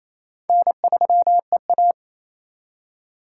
Morse code, heard as a keyed tone, about 36 wpm, 700 Hz. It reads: N3EA